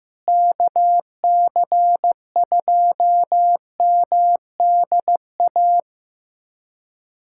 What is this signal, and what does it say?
Morse code, heard as a keyed tone, about 15 wpm, 700 Hz. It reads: KC2MDA
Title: USA Callsign DE Callsign - 6 Characters - 15wpm